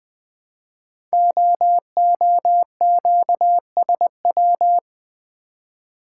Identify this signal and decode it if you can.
Morse code, heard as a keyed tone, about 20 wpm, 700 Hz. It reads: OOQSW
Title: Code Groups - 5 Letters - 20wpm